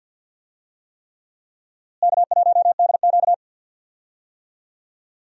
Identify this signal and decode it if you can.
Morse code, heard as a keyed tone, about 49 wpm, 700 Hz. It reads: K1DQ
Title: USA Call Signs - 4 Characters - Repeated 3x - 50wpm